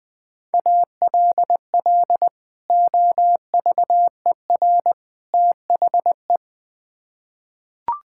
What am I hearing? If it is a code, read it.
Morse code, heard as a keyed tone, about 21 wpm, 700 Hz. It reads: ALL OVER THE
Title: Build A Phrase - 3-Word Phrases - 20wpm